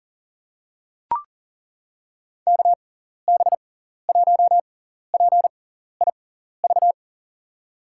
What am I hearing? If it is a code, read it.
Morse code, heard as a keyed tone, about 40 wpm, spaced as for 14 wpm, 700 Hz. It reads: KB1PIV
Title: USA Call Signs - 6 Characters - 6x Character Spacing - 40wpm